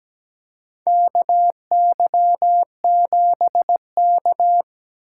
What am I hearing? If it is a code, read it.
Morse code, heard as a keyed tone, about 17 wpm, 700 Hz. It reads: KY7K
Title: K1USN SST QSO Practice - 17wpm